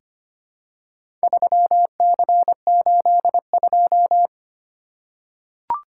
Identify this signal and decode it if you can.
Morse code, heard as a keyed tone, about 25 wpm, 700 Hz. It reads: 3C82
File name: practice / 25wpm / Four Letter-Number Combinations - 25wpm